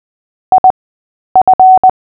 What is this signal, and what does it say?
Morse code, heard as a keyed tone, about 20 wpm, spaced as for 10 wpm, 750 Hz. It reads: IF
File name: cw / fists / Short-Story-8-10wpm